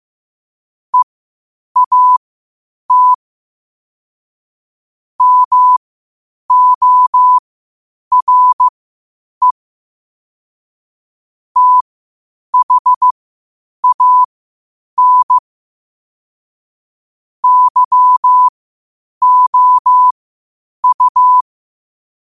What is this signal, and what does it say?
Morse code, heard as a keyed tone, about 15 wpm, spaced as for 8 wpm, 1000 Hz. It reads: EAT MORE THAN YOU
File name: qotdfiles / 8wpm